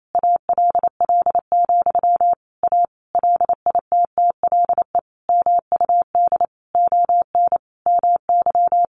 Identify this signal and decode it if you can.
Morse code, heard as a keyed tone, about 28 wpm, 700 Hz. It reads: ALL, A LITTLE MUD ON MY
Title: A Sound Of Thunder 28wpm